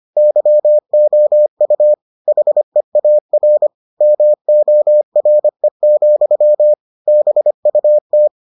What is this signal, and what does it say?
Morse code, heard as a keyed tone, about 25 wpm, 600 Hz. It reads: YOU HEAR MORE, BUT